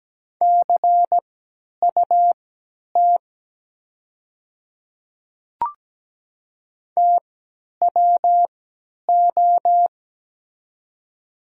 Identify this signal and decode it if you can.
Morse code, heard as a keyed tone, about 17 wpm, spaced as for 10 wpm, 700 Hz. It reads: CUT TWO
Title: 3x Character Spacing - Top 300 Words - 17wpm